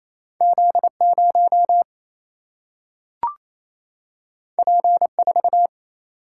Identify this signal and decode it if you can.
Morse code, heard as a keyed tone, about 28 wpm, 700 Hz. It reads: Z 0 P 4
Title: Single-Letter Single-Number 28wpm